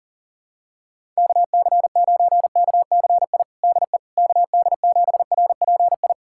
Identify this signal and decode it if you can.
Morse code, heard as a keyed tone, about 40 wpm, 700 Hz. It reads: KC9KCI DE KD7RPI
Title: USA Callsign DE Callsign 40wpm